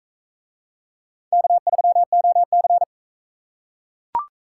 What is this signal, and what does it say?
Morse code, heard as a keyed tone, about 41 wpm, 700 Hz. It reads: K3OC